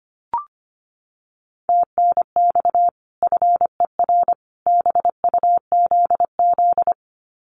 Morse code, heard as a keyed tone, about 25 wpm, 700 Hz. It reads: TNX FER BUZZ
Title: Common QSO Phrases 25wpm